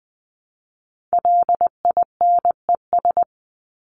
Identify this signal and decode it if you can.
Morse code, heard as a keyed tone, about 20 wpm, 700 Hz. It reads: LINES
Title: ES:S Suffix - Top 100 Words with Base Words 20wpm